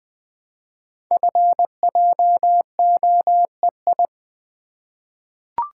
Morse code, heard as a keyed tone, about 20 wpm, 700 Hz. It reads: FJOEI
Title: Code Groups - 5 Letters - 20wpm